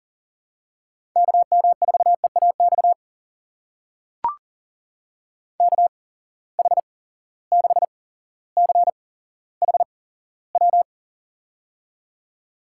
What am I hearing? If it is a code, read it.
Morse code, heard as a keyed tone, about 37 wpm, spaced as for 17 wpm, 700 Hz. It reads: KM4EAX KH6CHW
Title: USA Call Signs - 6 Characters - 8x Character Spacing - 40wpm